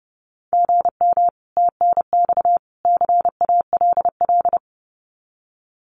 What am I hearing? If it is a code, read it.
Morse code, heard as a keyed tone, about 30 wpm, 700 Hz. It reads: GM TNX CALL
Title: Common QSO Phrases 30wpm